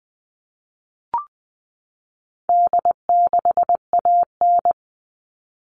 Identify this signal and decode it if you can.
Morse code, heard as a keyed tone, about 20 wpm, 700 Hz. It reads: D6AN
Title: All Cut Numbers - 4-Digit 20wpm